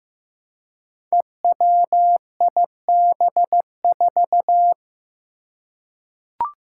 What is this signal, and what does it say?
Morse code, heard as a keyed tone, about 15 wpm, 700 Hz. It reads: EWIB4
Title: Five Challenging Letter-Number Combinations 15wpm